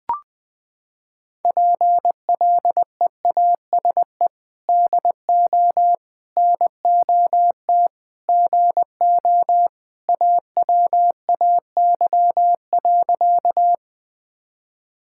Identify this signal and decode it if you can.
Morse code, heard as a keyed tone, about 20 wpm, 700 Hz. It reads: PLEASE DO NOT GO AWAY.
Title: Sentences with 4-Word Phrases - 20wpm